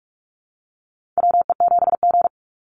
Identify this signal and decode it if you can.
Morse code, heard as a keyed tone, about 44 wpm, 700 Hz. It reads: WE7G
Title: Call Signs Repeated 3x - 45wpm